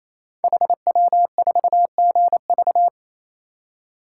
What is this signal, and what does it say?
Morse code, heard as a keyed tone, about 27 wpm, 700 Hz. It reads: HW4GV
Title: Code Groups - 5 Letter:Numbers - 28wpm